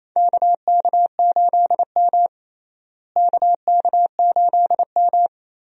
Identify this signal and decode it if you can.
Morse code, heard as a keyed tone, about 27 wpm, 700 Hz. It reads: KK8M KK8M
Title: USA Call Signs - 4 Characters - Repeated 3x - 28wpm